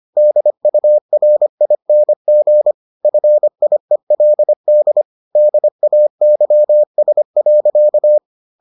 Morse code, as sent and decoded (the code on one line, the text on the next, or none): -.. ..- .-. .. -. --. / ..-. .. .-.. -.. / -.. .- -.-- ... .-.-.-
DURING FIELD DAYS.